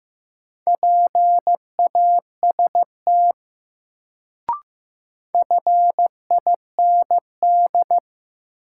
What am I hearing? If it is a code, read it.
Morse code, heard as a keyed tone, about 15 wpm, 700 Hz. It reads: PAST FIND
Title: Sets of 2 Words - New Word Drill - 15wpm